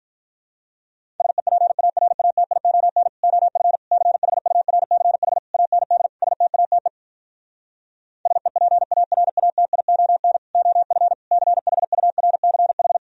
Morse code, heard as a keyed tone, about 46 wpm, 700 Hz. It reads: SEPARATION OF CHURCH AND STATE SEPARATION OF CHURCH